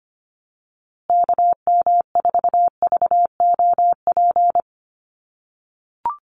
Morse code, heard as a keyed tone, about 25 wpm, 700 Hz. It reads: KM4VOP
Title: USA Call Signs Repeated 2x 25wpm